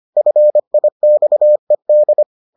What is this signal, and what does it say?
Morse code, heard as a keyed tone, about 25 wpm, 600 Hz. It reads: FIXED